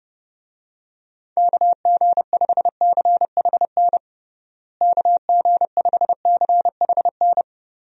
Morse code, heard as a keyed tone, about 32 wpm, 700 Hz. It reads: KG5CHN KG5CHN
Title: USA Call Signs - 6 Characters - Repeated 3x - 30wpm